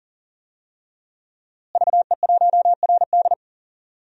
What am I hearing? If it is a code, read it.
Morse code, heard as a keyed tone, about 40 wpm, 700 Hz. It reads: VE1RD
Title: DXCC-100 Call Signs - 5 Characters - Repeated 3x - 40wpm